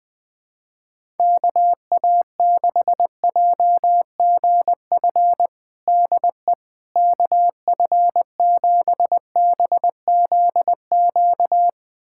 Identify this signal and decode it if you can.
Morse code, heard as a keyed tone, about 20 wpm, 700 Hz. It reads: KA6JGF DE KF7BZQ
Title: USA Callsign DE Callsign - 6 Characters - 20wpm